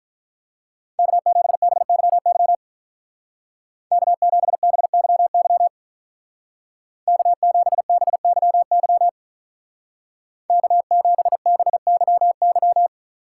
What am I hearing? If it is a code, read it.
Morse code, heard as a keyed tone, about 43 wpm, 700 Hz. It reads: K7BYY K7BYY K7BYY K7BYY
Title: Speed Racing - USA 1x3 Call Signs 35wpm